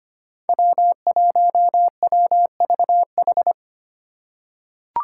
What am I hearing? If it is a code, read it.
Morse code, heard as a keyed tone, about 25 wpm, 700 Hz. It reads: W1WVH
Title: Five Challenging Letter-Number Combinations 25wpm